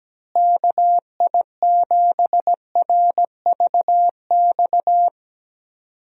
Morse code, heard as a keyed tone, about 17 wpm, 700 Hz. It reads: KI7RVX
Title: USA Call Signs - 6 Characters - Repeated 2x - 17wpm